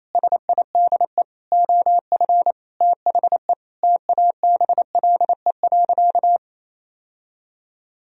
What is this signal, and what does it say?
Morse code, heard as a keyed tone, about 27 wpm, 700 Hz. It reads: SIDE OF THE TABLE.
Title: Sentences from Top 1000 Words - Review Part 4 - 28wpm